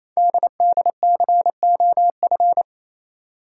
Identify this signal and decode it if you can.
Morse code, heard as a keyed tone, about 29 wpm, 700 Hz. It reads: DDCOF
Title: Code Groups - 5 Letters - 28wpm